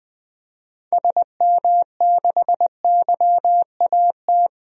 Morse code, heard as a keyed tone, about 20 wpm, 700 Hz. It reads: SM6YAT
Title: DXCC-100 Call Signs - 6 Characters - Repeated 2x - 20wpm